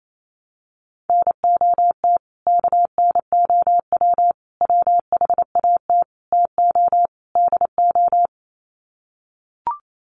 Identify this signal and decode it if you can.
Morse code, heard as a keyed tone, about 28 wpm, 700 Hz. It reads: NOT KNOW WHAT TO DO
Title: Sets of 5 Words 28wpm